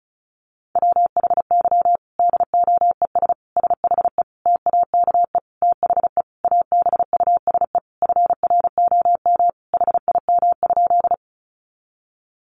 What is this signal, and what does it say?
Morse code, heard as a keyed tone, about 34 wpm, 700 Hz. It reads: WHY DOES SHE TAKE THE ABUSE FROM HIM?
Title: Sentences from Top 1300 Words 35wpm